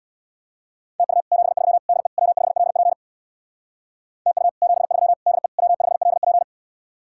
Morse code, heard as a keyed tone, about 66 wpm, 700 Hz. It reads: TU 73 DE W5LP TU 73 DE W5LP